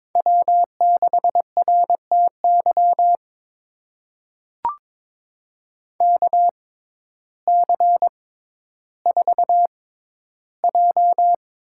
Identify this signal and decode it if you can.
Morse code, heard as a keyed tone, about 22 wpm, spaced as for 12 wpm, 700 Hz. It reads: W6RTY KC4J
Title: USA Call Signs - 6x Character Spacing - 22wpm